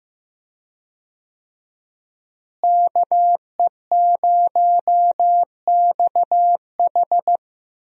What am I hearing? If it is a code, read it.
Morse code, heard as a keyed tone, about 15 wpm, 700 Hz. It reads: KE0XH
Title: USA Call Signs - 5 Characters - Repeated 2x - 15wpm